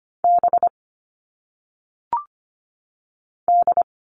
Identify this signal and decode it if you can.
Morse code, heard as a keyed tone, about 25 wpm, 700 Hz. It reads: B D
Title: All Cut Numbers - 1-Digit 25wpm